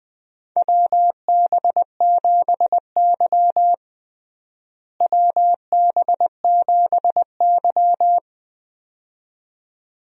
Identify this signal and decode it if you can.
Morse code, heard as a keyed tone, about 20 wpm, 700 Hz. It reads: WB7Y WB7Y
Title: USA Call Signs - 4 Characters - Repeated 2x - 20wpm